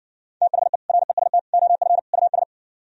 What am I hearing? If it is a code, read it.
Morse code, heard as a keyed tone, about 59 wpm, 700 Hz. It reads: THE REST OF US